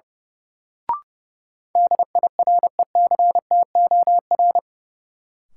Morse code, heard as a keyed tone, about 30 wpm, 700 Hz. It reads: DIRECTOR
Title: Sentences from Top 1300 Words - New Word Drill Alt - 30wpm